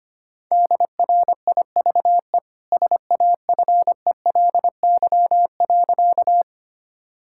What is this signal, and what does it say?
Morse code, heard as a keyed tone, about 25 wpm, 700 Hz. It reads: DRIVE SAFELY.
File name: Easy Sentences with Hints - 25wpm